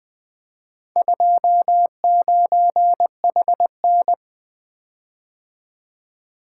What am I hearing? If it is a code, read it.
Morse code, heard as a keyed tone, about 20 wpm, 700 Hz. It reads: 29HN
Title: Four Letter-Number Combinations - 20wpm